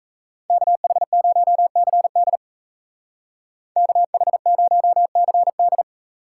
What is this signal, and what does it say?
Morse code, heard as a keyed tone, about 36 wpm, 700 Hz. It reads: KH0CD KH0CD